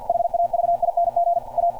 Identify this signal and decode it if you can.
Morse code, heard as a keyed tone, about 46 wpm, 690 Hz. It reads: HAVING A